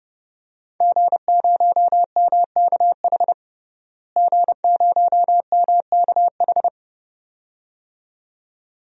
Morse code, heard as a keyed tone, about 30 wpm, 700 Hz. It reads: G0MKH G0MKH
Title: DXCC-100 Call Signs - 5 Characters - Repeated 2x - 30wpm